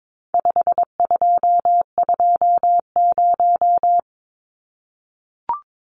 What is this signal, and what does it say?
Morse code, heard as a keyed tone, about 22 wpm, 700 Hz. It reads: 5220
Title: Numbers - 4-Digit 22wpm